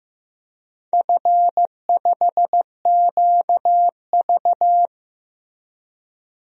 Morse code, heard as a keyed tone, about 15 wpm, 700 Hz. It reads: F5QV
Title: DXCC-100 Call Signs - 4 Characters - Repeated 3x - 15wpm